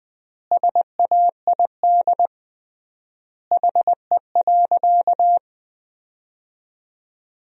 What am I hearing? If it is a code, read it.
Morse code, heard as a keyed tone, about 21 wpm, 700 Hz. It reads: SAID HE.
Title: 3x Word-Spacing - Sentences from Top 100 Words - 20wpm